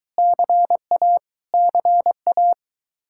Morse code, heard as a keyed tone, about 22 wpm, 700 Hz. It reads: CA CA